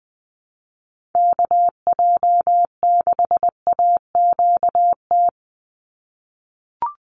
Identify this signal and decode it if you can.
Morse code, heard as a keyed tone, about 20 wpm, 700 Hz. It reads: KJ6AQT
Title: USA Call Signs 20wpm